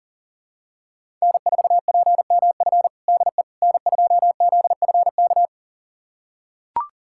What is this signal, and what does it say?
Morse code, heard as a keyed tone, about 40 wpm, 700 Hz. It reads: N4PMF DE N2ZFK
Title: USA Callsign DE Callsign - 5 Characters - 40wpm